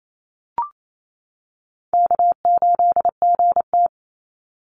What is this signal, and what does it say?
Morse code, heard as a keyed tone, about 28 wpm, 700 Hz. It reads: K8GT
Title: Call Signs - 28wpm